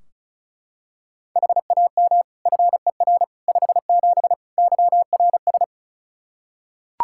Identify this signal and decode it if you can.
Morse code, heard as a keyed tone, about 36 wpm, 700 Hz. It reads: HAM FER 57 YRS
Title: K1USN SST QSO Practice 35wpm